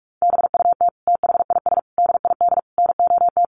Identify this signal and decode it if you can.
Morse code, heard as a keyed tone, about 44 wpm, 700 Hz. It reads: BUT THIS DID NOT
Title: Sentences from Top 1200 Words 45wpm